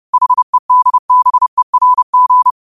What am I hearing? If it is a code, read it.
Morse code, heard as a keyed tone, about 30 wpm, 1000 Hz. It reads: HEDBERG
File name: qotdfiles / 30wpm